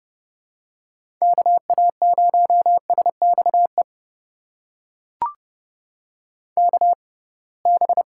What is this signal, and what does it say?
Morse code, heard as a keyed tone, about 32 wpm, spaced as for 23 wpm, 700 Hz. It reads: KA0SXE KB
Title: USA Call Signs - 6x Character Spacing - 30wpm